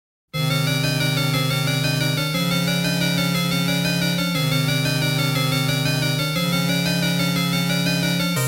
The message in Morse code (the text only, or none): none